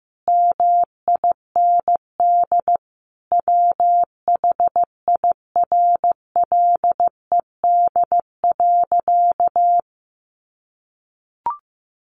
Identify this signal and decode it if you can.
Morse code, heard as a keyed tone, about 15 wpm, 700 Hz. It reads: MIND WHIRLED.
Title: A Sound Of Thunder 15wpm